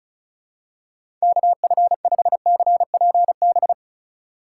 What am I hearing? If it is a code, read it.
Morse code, heard as a keyed tone, about 36 wpm, 700 Hz. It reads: KF5CPB
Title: USA Call Signs - 6 Characters - 2x Character Spacing - 35wpm